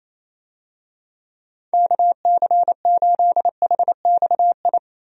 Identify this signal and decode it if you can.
Morse code, heard as a keyed tone, about 28 wpm, 700 Hz. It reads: KC8HXI
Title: USA Call Signs - 6 Characters - 28wpm